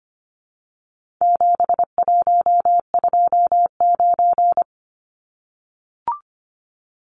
Morse code, heard as a keyed tone, about 25 wpm, 700 Hz. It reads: 7129
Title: Numbers - 4-Digit 25wpm